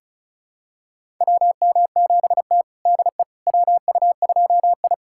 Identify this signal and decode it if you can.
Morse code, heard as a keyed tone, about 36 wpm, 700 Hz. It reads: WM7T DE WU2I
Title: USA Callsign DE Callsign - 4 Characters - 35wpm